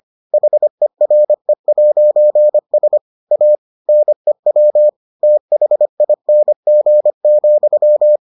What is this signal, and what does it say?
Morse code, heard as a keyed tone, about 25 wpm, 600 Hz. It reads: HERE'S A NEW THING,